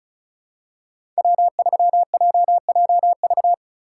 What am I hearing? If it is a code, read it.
Morse code, heard as a keyed tone, about 36 wpm, 700 Hz. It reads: W3JJV
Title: USA Call Signs - 5 Characters - 4x Character Spacing - 35wpm